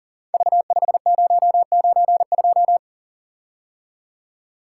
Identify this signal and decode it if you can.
Morse code, heard as a keyed tone, about 42 wpm, 690 Hz. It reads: V5092
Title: Code Groups - 5 Letter:Numbers - 40wpm